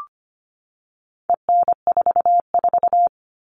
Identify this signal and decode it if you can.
Morse code, heard as a keyed tone, about 25 wpm, 700 Hz. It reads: EN44
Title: All Cut Numbers - 4-Digit 25wpm